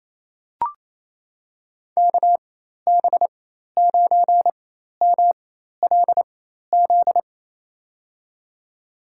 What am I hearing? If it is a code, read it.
Morse code, heard as a keyed tone, about 27 wpm, spaced as for 13 wpm, 700 Hz. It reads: KB9MLZ
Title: USA Call Signs - 6 Characters - 4x Character Spacing - 28wpm